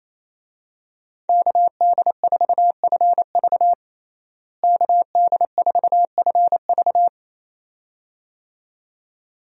Morse code, heard as a keyed tone, about 28 wpm, 700 Hz. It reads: KD4FV KD4FV
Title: USA Call Signs - 5 Characters - Repeated 2x - 28wpm